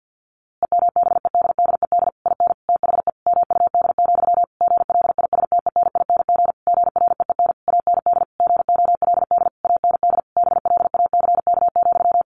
Morse code, heard as a keyed tone, about 50 wpm, 700 Hz. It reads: EMBEDDED IN THE MUD, GLISTENING GREEN AND GOLD AND BLACK,
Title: A Sound Of Thunder 50wpm